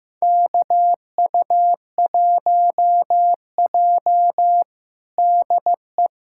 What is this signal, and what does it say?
Morse code, heard as a keyed tone, about 15 wpm, 700 Hz. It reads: KU1J DE
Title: USA Callsign DE Callsign - 4 Characters - 15wpm